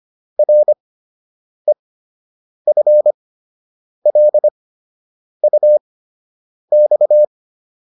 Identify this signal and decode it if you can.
Morse code, heard as a keyed tone, about 25 wpm, spaced as for 8 wpm, 600 Hz. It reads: REFLUX